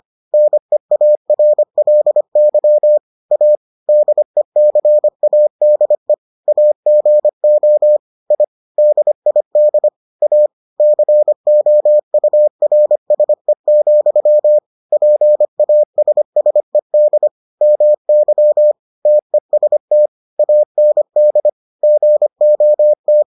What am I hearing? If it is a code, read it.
Morse code, heard as a keyed tone, about 25 wpm, 600 Hz. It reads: NEARLY A DECADE AGO I DID A COURSE, PASSED MY TEST AND GOT